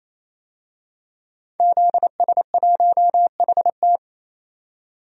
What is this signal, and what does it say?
Morse code, heard as a keyed tone, about 28 wpm, 700 Hz. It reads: ZS1HT
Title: DXCC-100 Call Signs - 5 Characters - 28wpm